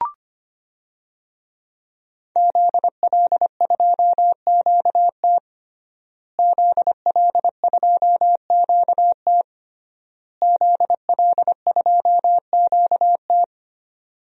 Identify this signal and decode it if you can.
Morse code, heard as a keyed tone, about 25 wpm, 700 Hz. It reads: ZL2QT ZL2QT ZL2QT
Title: DXCC-100 Call Signs - 5 Characters - Repeated 3x - 25wpm